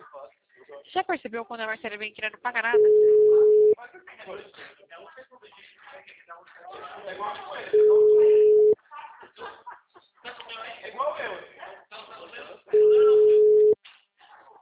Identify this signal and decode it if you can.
Ringback tone (Europe, 425 Hz), with speech.